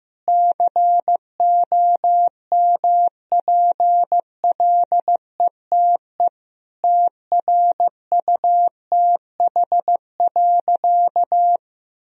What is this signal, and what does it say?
Morse code, heard as a keyed tone, about 15 wpm, 700 Hz. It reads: COMPLETE TRUTH.
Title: Sentences from Top 1000 Words - Review Part 2 - 15wpm